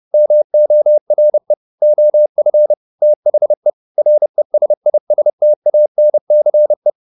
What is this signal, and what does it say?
Morse code, heard as a keyed tone, about 30 wpm, 600 Hz. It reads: MORE OF THE RESISTANCE